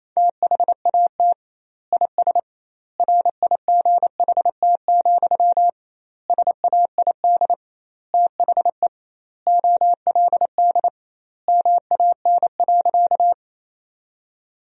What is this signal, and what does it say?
Morse code, heard as a keyed tone, about 27 wpm, 700 Hz. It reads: THAT IS RIGHT, SAID THE OLD MAN.
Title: 2x Word-Spacing - Sentences from Top 200 Words - 28wpm